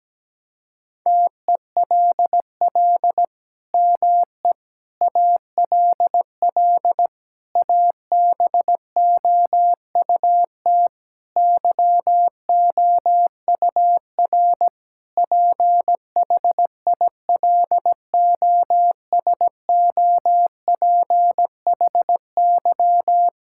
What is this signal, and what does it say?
Morse code, heard as a keyed tone, about 17 wpm, 700 Hz. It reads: TELL ME ALL ABOUT YOUR PHILOSOPHY
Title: Sentences from Top 1000 Words - Review Part 6 - 17wpm